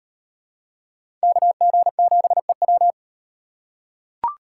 Morse code, heard as a keyed tone, about 36 wpm, 700 Hz. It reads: KG7EW